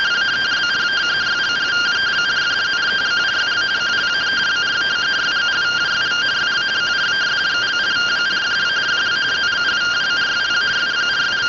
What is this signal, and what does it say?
Two-tone teleprinter signal, about 170 Hz shift.